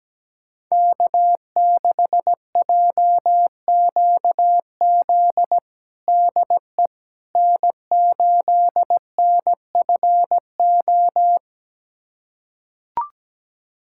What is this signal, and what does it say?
Morse code, heard as a keyed tone, about 17 wpm, 700 Hz. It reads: K6JQZ DE N8NFO
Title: USA Callsign DE Callsign - 5 Characters - 17wpm